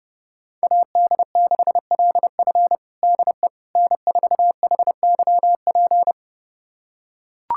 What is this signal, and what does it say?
Morse code, heard as a keyed tone, about 32 wpm, 700 Hz. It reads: AD6LF DE N4HYP
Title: USA Callsign DE Callsign - 5 Characters - 30wpm